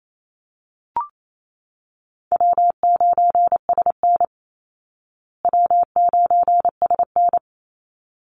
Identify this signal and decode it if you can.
Morse code, heard as a keyed tone, about 28 wpm, 700 Hz. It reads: W9SN W9SN
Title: Easy Call Signs Repeated 2x - 28wpm